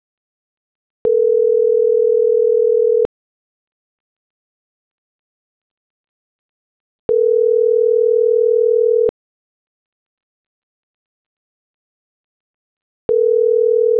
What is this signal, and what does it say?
Ringback tone (North America), 440 + 480 Hz.